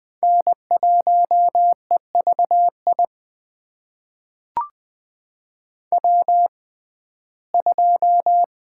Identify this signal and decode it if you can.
Morse code, heard as a keyed tone, about 21 wpm, spaced as for 15 wpm, 700 Hz. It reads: N1EVI W2
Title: USA Call Signs - 5 Characters - 6x Character Spacing - 20wpm